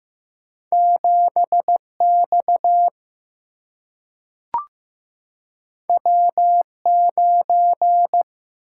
Morse code, heard as a keyed tone, about 15 wpm, 700 Hz. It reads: 7 X W 9
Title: Single-Letter Single-Number 15wpm